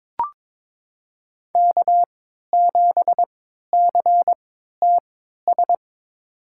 Morse code, heard as a keyed tone, about 22 wpm, spaced as for 12 wpm, 700 Hz. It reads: K7CTS
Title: USA Call Signs - 5 Characters - 3x Character Spacing - 22wpm